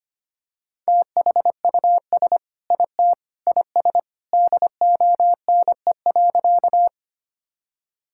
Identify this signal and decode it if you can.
Morse code, heard as a keyed tone, about 25 wpm, 700 Hz. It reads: THUS IT IS DONE.
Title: Sentences from Top 1000 Words - Review Part 5 - 25wpm